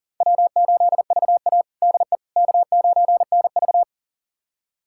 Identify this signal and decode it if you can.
Morse code, heard as a keyed tone, about 40 wpm, 700 Hz. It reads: W8VA DE K9NV
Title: USA Callsign DE Callsign - 4 Characters - 40wpm